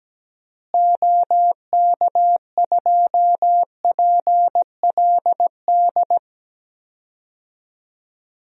Morse code, heard as a keyed tone, about 17 wpm, 700 Hz. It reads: OK2PLD
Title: DXCC-100 Call Signs - 6 Characters - Repeated 3x - 17wpm